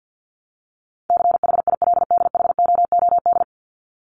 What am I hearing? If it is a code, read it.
Morse code, heard as a keyed tone, about 49 wpm, 700 Hz. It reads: CHILDHOOD
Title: Top 1751-2000 Words 50wpm